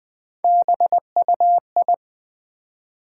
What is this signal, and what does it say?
Morse code, heard as a keyed tone, about 20 wpm, 700 Hz. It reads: BUI
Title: Build A Word - 3 Letters At A Time - 8-Letter Words - 20wpm